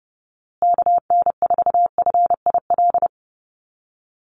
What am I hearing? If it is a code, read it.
Morse code, heard as a keyed tone, about 30 wpm, 700 Hz. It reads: KN4FIL
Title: USA Call Signs Repeated 3x 30wpm